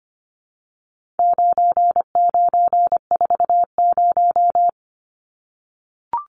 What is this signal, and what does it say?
Morse code, heard as a keyed tone, about 24 wpm, 700 Hz. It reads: 9940
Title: Numbers - 4-Digit 25wpm